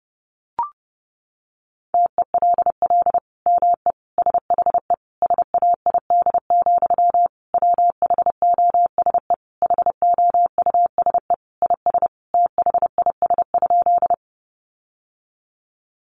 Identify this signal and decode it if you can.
Morse code, heard as a keyed tone, about 30 wpm, 700 Hz. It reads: TELL ME SHE SAID, WHOSE HOUSE IS THIS?
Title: Sentences from Top 1200 Words 30wpm